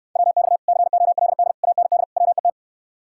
Morse code, heard as a keyed tone, about 69 wpm, 700 Hz. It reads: WX COLD AND WET